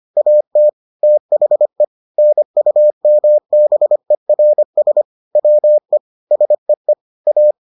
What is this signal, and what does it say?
Morse code, heard as a keyed tone, about 25 wpm, 600 Hz. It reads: AT THE NUMBERS WE SEE A